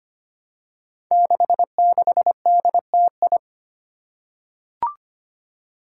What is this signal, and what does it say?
Morse code, heard as a keyed tone, about 25 wpm, 700 Hz. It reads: 66DTI
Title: Five Challenging Letter-Number Combinations 25wpm